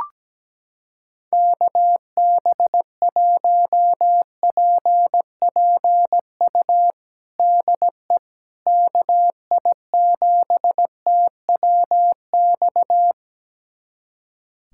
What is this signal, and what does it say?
Morse code, heard as a keyed tone, about 17 wpm, 700 Hz. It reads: KB1PPU DE KI7TWX
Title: USA Callsign DE Callsign - 6 Characters - 17wpm